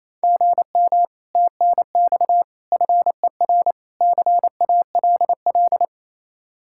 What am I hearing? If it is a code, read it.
Morse code, heard as a keyed tone, about 35 wpm, 700 Hz. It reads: GM TNX FER CALL